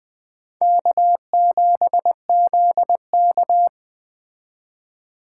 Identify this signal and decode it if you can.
Morse code, heard as a keyed tone, about 20 wpm, 700 Hz. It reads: K7ZK
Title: Speed Racing - USA 1x2 Call Signs 17wpm